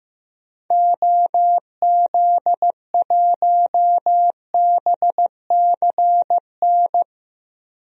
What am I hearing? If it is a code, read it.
Morse code, heard as a keyed tone, about 15 wpm, 700 Hz. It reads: OZ1BCN
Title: DXCC-100 Call Signs - 6 Characters - Repeated 3x - 15wpm